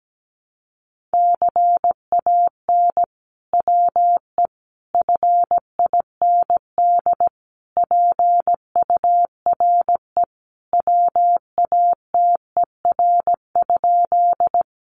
Morse code, heard as a keyed tone, about 17 wpm, 700 Hz. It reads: CAN WE FIND PURE WATER?
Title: Sentences from Top 1500 Words 17wpm